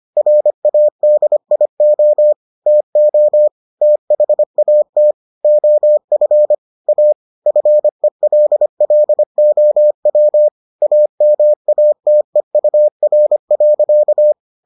Morse code, heard as a keyed tone, about 25 wpm, 600 Hz. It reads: RADIO TO THAT OF A FELLOW AMATEUR.